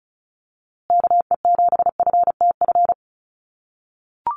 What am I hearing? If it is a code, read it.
Morse code, heard as a keyed tone, about 34 wpm, 700 Hz. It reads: KE7FTF
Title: USA Call Signs Repeated 2x 35wpm